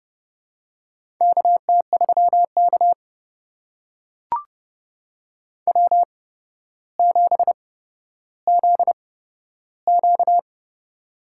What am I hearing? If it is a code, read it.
Morse code, heard as a keyed tone, about 30 wpm, spaced as for 13 wpm, 700 Hz. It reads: KT3K W7ZQ
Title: USA Call Signs - 4 Characters - 8x Character Spacing - 30wpm